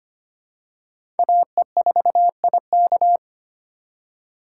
Morse code, heard as a keyed tone, about 25 wpm, 700 Hz. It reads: AE4IK
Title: USA Call Signs - 5 Characters - 2x Character Spacing - 25wpm